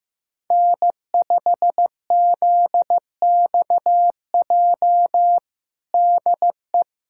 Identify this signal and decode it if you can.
Morse code, heard as a keyed tone, about 15 wpm, 700 Hz. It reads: N5ZXJ DE
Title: USA Callsign DE Callsign - 5 Characters - 15wpm